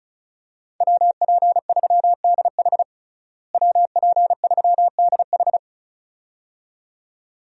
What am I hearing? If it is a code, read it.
Morse code, heard as a keyed tone, about 34 wpm, 700 Hz. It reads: WP3DH WP3DH
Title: USA Call Signs - 5 Characters - Repeated 2x - 35wpm